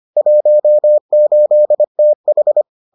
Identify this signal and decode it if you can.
Morse code, heard as a keyed tone, about 25 wpm, 600 Hz. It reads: 18TH